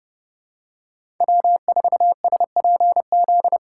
Morse code, heard as a keyed tone, about 29 wpm, 700 Hz. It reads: W4SPZ
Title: USA Call Signs - 5 Characters - 4x Character Spacing - 30wpm